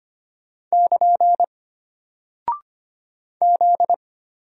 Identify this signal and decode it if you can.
Morse code, heard as a keyed tone, about 25 wpm, 700 Hz. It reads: <KN> Z
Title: Single Character (Letter:Number:Punctuation:Prosign) Alt 25wpm